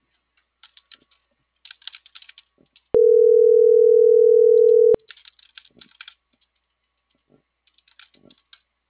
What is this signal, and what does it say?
Ringback tone (North America), 440 + 480 Hz.